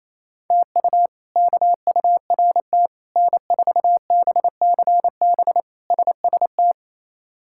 Morse code, heard as a keyed tone, about 28 wpm, 700 Hz. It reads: TU KURT N4BCB SST
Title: K1USN SST QSO Practice - 28wpm